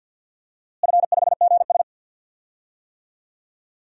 Morse code, heard as a keyed tone, about 49 wpm, 700 Hz. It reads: F5GS
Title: DXCC-100 Call Signs - 4 Characters - Repeated 2x - 50wpm